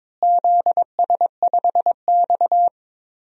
Morse code, heard as a keyed tone, about 22 wpm, 700 Hz. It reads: ZS5X
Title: DXCC-100 Call Signs - 4 Characters - Repeated 3x - 22wpm